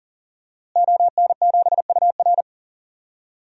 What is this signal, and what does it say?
Morse code, heard as a keyed tone, about 40 wpm, 700 Hz. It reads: ON7UR
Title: DXCC-100 Call Signs - 5 Characters - Repeated 3x - 40wpm